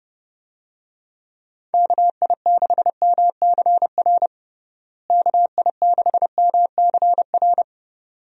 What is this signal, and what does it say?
Morse code, heard as a keyed tone, about 32 wpm, 700 Hz. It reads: KI6MCR KI6MCR
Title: USA Call Signs - 6 Characters - Repeated 2x - 30wpm